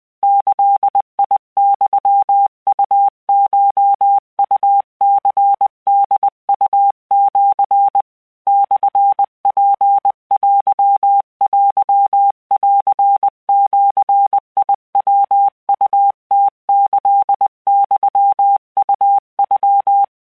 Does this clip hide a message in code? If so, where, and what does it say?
Yes: Morse code (Japanese Wabun code) from 0.2 to 20.0 s, heard as a keyed tone, about 20 wpm, 800 Hz. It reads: ギユウコウニホウシ モツテテンジヤウムキユウノ